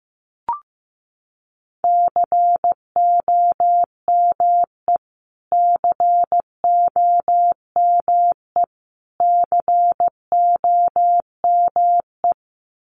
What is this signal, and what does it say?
Morse code, heard as a keyed tone, about 15 wpm, 700 Hz. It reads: COME COME COME
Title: Top 300 Words Repeated 15wpm